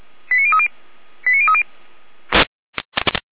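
Five-tone selective call (ZVEI), heard as quick runs of five tones, tones of about 70 ms. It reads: 89030 89030
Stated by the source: ZVEI five-tone selective call received by SDR.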